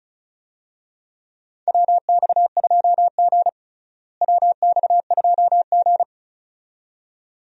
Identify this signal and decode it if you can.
Morse code, heard as a keyed tone, about 36 wpm, 700 Hz. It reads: WX2G WX2G
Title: USA Call Signs - 4 Characters - Repeated 2x - 35wpm